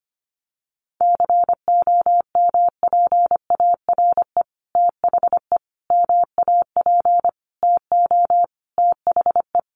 Morse code, heard as a keyed tone, about 25 wpm, 700 Hz. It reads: COMPARE THE MAP TO THE
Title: Sentences from Top 1300 Words 25wpm